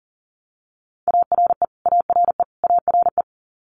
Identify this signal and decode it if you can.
Morse code, heard as a keyed tone, about 40 wpm, 700 Hz. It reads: ARE ARE ARE
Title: Top 500 Words Repeated 40wpm